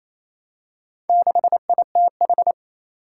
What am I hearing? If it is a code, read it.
Morse code, heard as a keyed tone, about 28 wpm, 700 Hz. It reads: 6ITH
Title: Four Challenging Letter-Number Combinations 28wpm